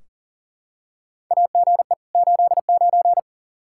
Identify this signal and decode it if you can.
Morse code, heard as a keyed tone, about 40 wpm, 700 Hz. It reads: AGE 89
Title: Practice QSO - Exchange 3 - More Info - 40wpm